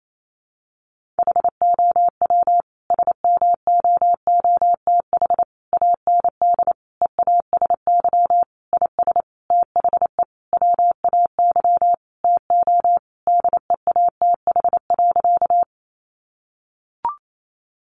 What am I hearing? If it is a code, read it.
Morse code, heard as a keyed tone, about 28 wpm, 700 Hz. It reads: HOW SMOOTH AND EASY IS THE WAY TO DEATH.
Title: Sentences from Top 1400 Words 28wpm